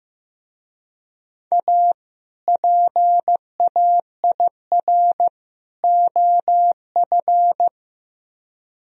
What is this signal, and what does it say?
Morse code, heard as a keyed tone, about 17 wpm, 700 Hz. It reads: A PAIR OF
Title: Speed-Racing - Sets of 3 Words 15wpm